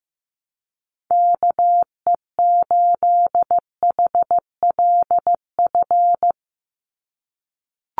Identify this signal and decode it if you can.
Morse code, heard as a keyed tone, about 15 wpm, 700 Hz. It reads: KE8HLF
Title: USA Call Signs 15wpm